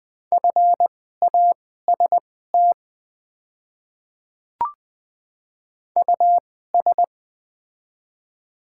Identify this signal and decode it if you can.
Morse code, heard as a keyed tone, about 21 wpm, spaced as for 15 wpm, 700 Hz. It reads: FAST US
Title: 2x Character Spacing - Top 500 Words - 20wpm